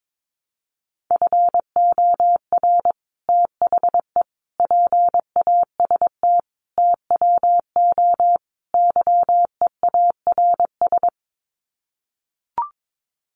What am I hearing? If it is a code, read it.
Morse code, heard as a keyed tone, about 22 wpm, 700 Hz. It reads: FOR THE PAST TWO YEARS
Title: Sets of 5 Words Encore2 22wpm